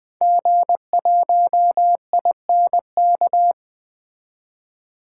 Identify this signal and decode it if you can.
Morse code, heard as a keyed tone, about 20 wpm, 700 Hz. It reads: G1INK
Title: 2023 SOTA Activator Call Signs - 20wpm